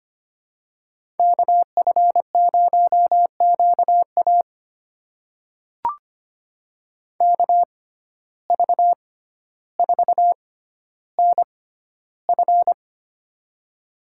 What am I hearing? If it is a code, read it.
Morse code, heard as a keyed tone, about 25 wpm, spaced as for 13 wpm, 700 Hz. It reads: KF0QA KV4NF
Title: USA Call Signs - 5 Characters - 6x Character Spacing - 25wpm